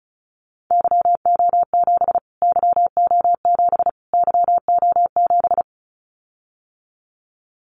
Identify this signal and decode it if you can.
Morse code, heard as a keyed tone, about 34 wpm, 700 Hz. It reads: YO7 YO7 YO7
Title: Callsign Prefixes for Top 25 DXCC - Repeated 3x - 35wpm